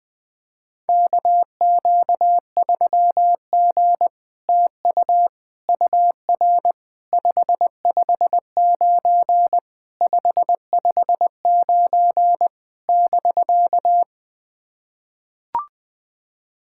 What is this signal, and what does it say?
Morse code, heard as a keyed tone, about 20 wpm, 700 Hz. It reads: KQ3G TU UR 559 559 <BK>
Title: SOTA QSO Practice 20wpm